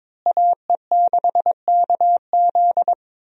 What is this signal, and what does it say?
Morse code, heard as a keyed tone, about 21 wpm, 700 Hz. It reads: AE6KZ
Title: USA Call Signs - 5 Characters - Repeated 2x - 22wpm